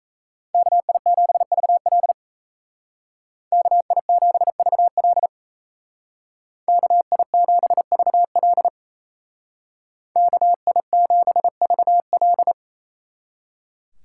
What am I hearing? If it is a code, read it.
Morse code, heard as a keyed tone, about 34 wpm, 700 Hz. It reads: KI7VL KI7VL KI7VL KI7VL
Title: Speed Racing - USA 2x2 Call Signs 28wpm